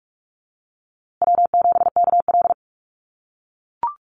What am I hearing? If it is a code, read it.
Morse code, heard as a keyed tone, about 44 wpm, 700 Hz. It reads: W7KL
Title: USA Call Signs Repeated 2x 45wpm